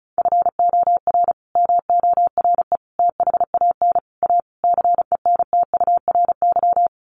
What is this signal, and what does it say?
Morse code, heard as a keyed tone, about 34 wpm, 700 Hz. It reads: FOR MORE THAN A CENTURY
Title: Sets of 5 Words Encore2 35wpm